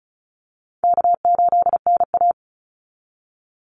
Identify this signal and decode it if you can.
Morse code, heard as a keyed tone, about 34 wpm, 700 Hz. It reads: K8NA
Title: Easy Call Signs Repeated 3x - 35wpm